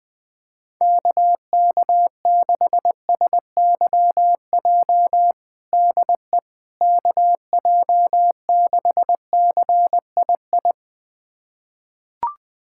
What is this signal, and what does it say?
Morse code, heard as a keyed tone, about 21 wpm, 700 Hz. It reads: KK6SYJ DE KJ6CII
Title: USA Callsign DE Callsign - 6 Characters - 20wpm